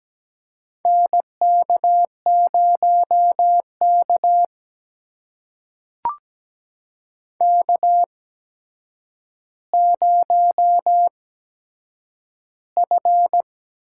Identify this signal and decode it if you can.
Morse code, heard as a keyed tone, about 17 wpm, spaced as for 8 wpm, 700 Hz. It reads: NK0K K0F
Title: USA Call Signs - 8x Character Spacing - 17wpm